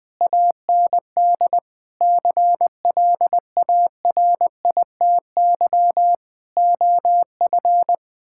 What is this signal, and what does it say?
Morse code, heard as a keyed tone, about 20 wpm, 700 Hz. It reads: AND CLARITY OF
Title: Difficult Sentences with Hints - 20wpm